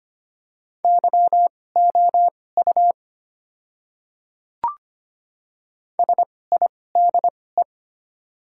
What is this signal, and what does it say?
Morse code, heard as a keyed tone, about 25 wpm, spaced as for 18 wpm, 700 Hz. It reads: YOU SIDE
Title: 2x Character Spacing - Top 100 Words - 25wpm